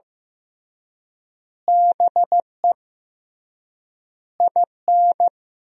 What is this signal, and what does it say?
Morse code, heard as a keyed tone, about 16 wpm, 700 Hz. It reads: BE IN
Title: 3x Word-Spacing - Sentences from Top 500 Words - 15wpm